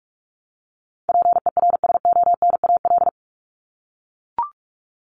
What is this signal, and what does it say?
Morse code, heard as a keyed tone, about 44 wpm, 700 Hz. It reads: PERSONAL
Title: Top 501-750 Words 45wpm